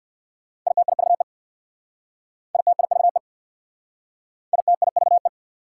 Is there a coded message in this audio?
Yes: Morse code from 0.7 to 5.3 s, heard as a keyed tone, about 64 wpm, 690 Hz. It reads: ITIVE ITIVE ITIVE